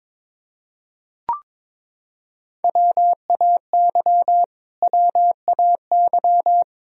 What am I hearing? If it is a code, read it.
Morse code, heard as a keyed tone, about 22 wpm, 700 Hz. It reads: WAY WAY
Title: Top 100 Words Repeated 22wpm